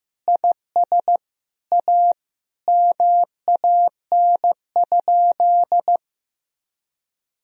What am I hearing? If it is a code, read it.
Morse code, heard as a keyed tone, about 15 wpm, 700 Hz. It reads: IS A MAN?
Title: Sentences from Top 1000 Words - Review Part 3 - 15wpm